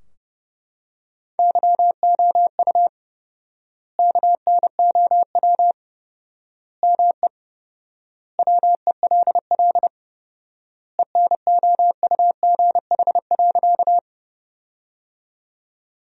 Morse code, heard as a keyed tone, about 32 wpm, 700 Hz. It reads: YOU KNOW ME WELL ENOUGH.
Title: 4x Word-Spacing - Sentences from Top 400 Words - 30wpm